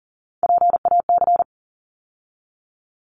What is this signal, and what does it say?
Morse code, heard as a keyed tone, about 40 wpm, 700 Hz. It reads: PAC
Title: ARRL Sections 40wpm